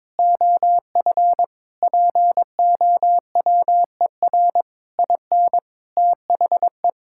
Morse code, heard as a keyed tone, about 22 wpm, 700 Hz. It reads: OF POWER IN THE
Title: Sentences from Top 1000 Words - Review Part 6 - 22wpm